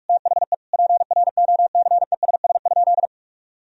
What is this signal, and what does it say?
Morse code, heard as a keyed tone, about 46 wpm, 700 Hz. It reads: THE PROCESS?
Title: Sentences from Top 1000 Words - Review Part 3 - 45wpm